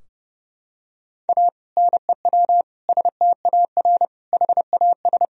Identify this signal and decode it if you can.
Morse code, heard as a keyed tone, about 31 wpm, 700 Hz. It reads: A NEW STAR HAS
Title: Sentences from Top 1000 Words - Review Part 2 - 30wpm